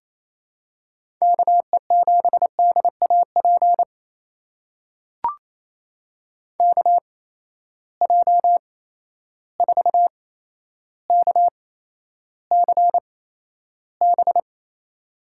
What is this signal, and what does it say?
Morse code, heard as a keyed tone, about 28 wpm, spaced as for 12 wpm, 700 Hz. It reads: KE7DAP KJ4KCB
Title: USA Call Signs - 6 Characters - 8x Character Spacing - 28wpm